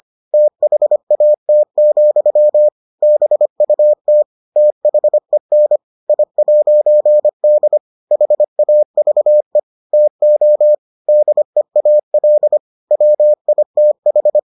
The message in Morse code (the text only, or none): THAT, BUT THEN I'D HAVE TO DEAL WITH